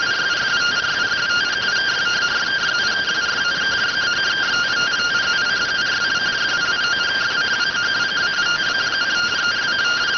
Two-tone teleprinter signal, about 170 Hz shift.